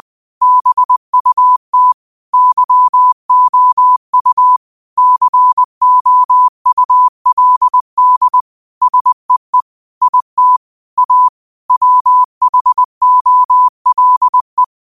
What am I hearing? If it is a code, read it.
Morse code, heard as a keyed tone, about 20 wpm, 1000 Hz. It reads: BUT YOU COULD SEE IT A WHOLE